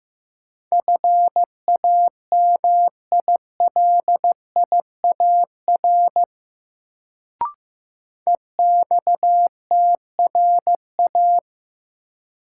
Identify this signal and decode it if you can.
Morse code, heard as a keyed tone, about 16 wpm, 700 Hz. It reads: FAMILIAR EXTRA
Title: Sentences from Top 1000 Words - New Word Drill - 15wpm